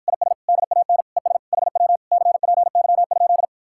Morse code, heard as a keyed tone, about 53 wpm, 700 Hz. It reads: IS DAN ES HW CPY?